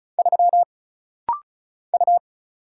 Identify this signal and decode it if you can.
Morse code, heard as a keyed tone, about 34 wpm, 700 Hz. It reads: 3 U
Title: Single Letter-Number Alt - Rapid-Fire 35wpm